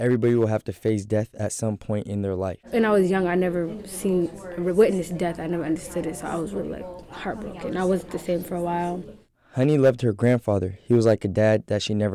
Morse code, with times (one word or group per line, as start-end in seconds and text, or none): none